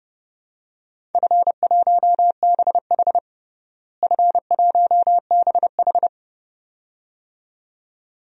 Morse code, heard as a keyed tone, about 30 wpm, 700 Hz. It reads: F1BH F1BH
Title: DXCC-100 Call Signs - 4 Characters - Repeated 2x - 30wpm